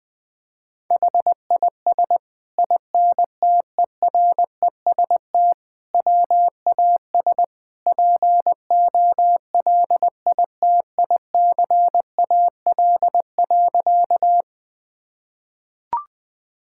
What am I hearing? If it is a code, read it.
Morse code, heard as a keyed tone, about 20 wpm, 700 Hz. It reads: HIS INTEREST WAS POLITICAL.
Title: Sentences from Top 1000 Words - Review Part 5 - 20wpm